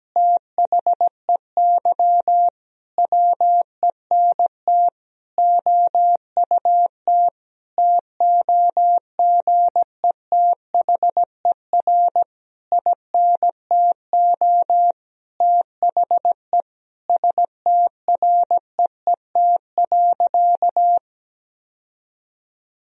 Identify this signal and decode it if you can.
Morse code, heard as a keyed tone, about 17 wpm, 700 Hz. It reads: THEY WENT OUT TOGETHER INTO THE STREET.
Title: Sentences from Top 1000 Words - Review Part 3 - 17wpm